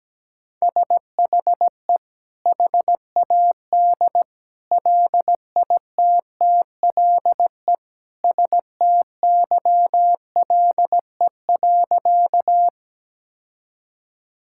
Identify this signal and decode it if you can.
Morse code, heard as a keyed tone, about 17 wpm, 700 Hz. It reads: SHE HAD LITTLE STYLE.
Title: Sentences from Top 1000 Words - Review Part 5 - 17wpm